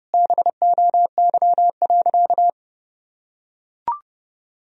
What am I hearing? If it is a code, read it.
Morse code, heard as a keyed tone, about 30 wpm, 700 Hz. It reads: BOY.
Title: Sentences from Top 1000 Words - Review Part 4 - 30wpm